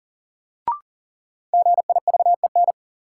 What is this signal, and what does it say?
Morse code, heard as a keyed tone, about 40 wpm, 700 Hz. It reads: GIVEN
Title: Sentences from Top 1400 Words - New Word Drill Alt - 40wpm